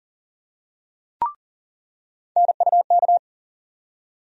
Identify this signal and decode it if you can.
Morse code, heard as a keyed tone, about 40 wpm, 700 Hz. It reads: NUK
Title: USA Call Sign Postfixes - 3 Letters 40wpm